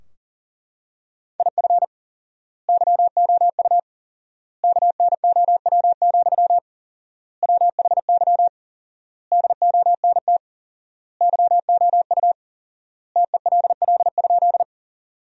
Morse code, heard as a keyed tone, about 40 wpm, 700 Hz. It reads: IF YOU KNOW, WHY DONT YOU TELL?
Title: 4x Word-Spacing - Sentences from Top 300 Words - 40wpm